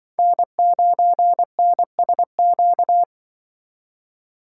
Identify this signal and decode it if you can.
Morse code, heard as a keyed tone, about 24 wpm, 700 Hz. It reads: N9NSQ
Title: Speed Racing - USA 1x3 Call Signs 20wpm